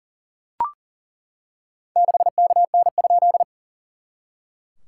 Morse code, heard as a keyed tone, about 40 wpm, 690 Hz. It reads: 6KN?
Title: SOTA QSO Practice 40wpm